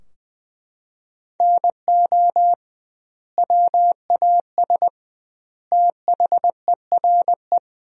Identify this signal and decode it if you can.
Morse code, heard as a keyed tone, about 20 wpm, 700 Hz. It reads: NO WAS THERE
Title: 2x Word-Spacing - Sentences from Top 100 Words - 20wpm